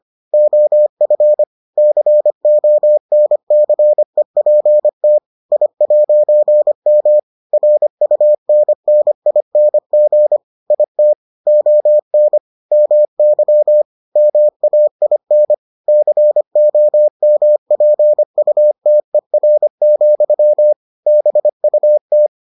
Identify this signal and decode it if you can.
Morse code, heard as a keyed tone, about 25 wpm, 600 Hz. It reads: OF CONCEPT I'M RUNNING IT ON MY MAIN COMPUTER, BUT